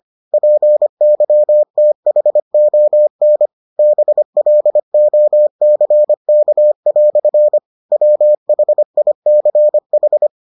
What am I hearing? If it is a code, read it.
Morse code, heard as a keyed tone, about 25 wpm, 600 Hz. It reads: PYTHON BLOCK" WHICH